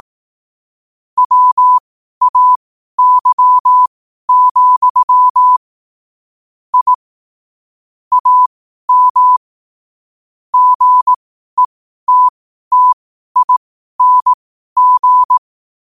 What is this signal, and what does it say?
Morse code, heard as a keyed tone, about 18 wpm, spaced as for 12 wpm, 1000 Hz. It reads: WAY, I AM GETTING
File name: qotdfiles / 12wpm